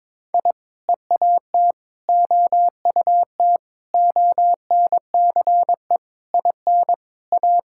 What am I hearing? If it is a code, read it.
Morse code, heard as a keyed tone, about 22 wpm, 700 Hz. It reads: I EAT OUT ONCE IN A
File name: practice / 22wpm / Sentences with 4-Word Phrases - 22wpm